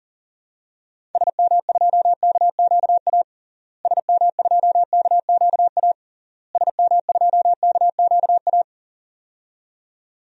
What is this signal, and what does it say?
Morse code, heard as a keyed tone, about 40 wpm, 700 Hz. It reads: SM2KQA SM2KQA SM2KQA
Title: DXCC-100 Call Signs - 6 Characters - Repeated 3x - 40wpm